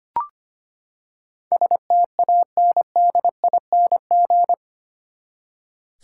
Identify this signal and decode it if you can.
Morse code, heard as a keyed tone, about 25 wpm, 700 Hz. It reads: STANDING
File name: Eight Letter Words - 25wpm